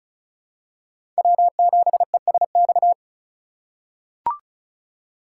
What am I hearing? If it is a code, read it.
Morse code, heard as a keyed tone, about 36 wpm, 700 Hz. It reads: W7ESX
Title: USA Call Signs - 5 Characters - 2x Character Spacing - 35wpm